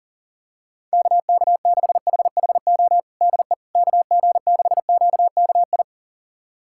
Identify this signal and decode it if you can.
Morse code, heard as a keyed tone, about 40 wpm, 700 Hz. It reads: KK6HHO DE KG6QKI
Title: USA Callsign DE Callsign 40wpm